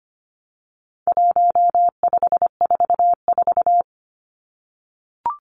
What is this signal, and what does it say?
Morse code, heard as a keyed tone, about 25 wpm, 700 Hz. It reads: 1544
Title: Numbers - 4-Digit 25wpm